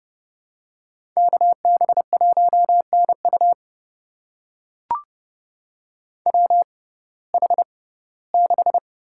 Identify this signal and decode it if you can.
Morse code, heard as a keyed tone, about 30 wpm, spaced as for 18 wpm, 700 Hz. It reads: KB1NU WH6
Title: USA Call Signs - 5 Characters - 6x Character Spacing - 30wpm